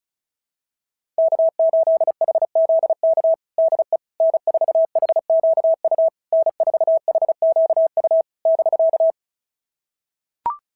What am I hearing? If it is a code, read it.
Morse code, heard as a keyed tone, about 33 wpm, 650 Hz. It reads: K8HZK DE N4HQU N4HQU <BK>
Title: SOTA QSO Practice 35wpm